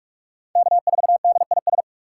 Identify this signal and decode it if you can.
Morse code, heard as a keyed tone, about 42 wpm, 700 Hz. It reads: K4DIS
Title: USA Call Signs - 5 Characters - 4x Character Spacing - 45wpm